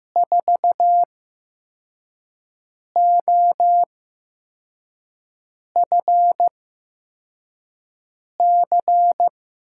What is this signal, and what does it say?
Morse code, heard as a keyed tone, about 15 wpm, spaced as for 4 wpm, 700 Hz. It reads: 4OFC
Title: USA Call Signs - 5 Characters - 8x Character Spacing - 15wpm